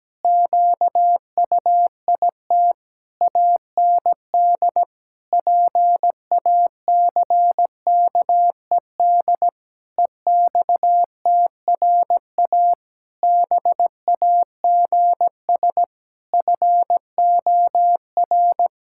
Morse code, heard as a keyed tone, about 17 wpm, 700 Hz. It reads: QUIT AND PACKED EXTRA BAGS FOR
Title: Pangram Challenge v2 - 17wpm